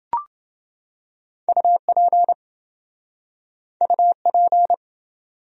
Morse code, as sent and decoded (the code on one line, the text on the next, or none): ..- .--. / ..- .--.
UP UP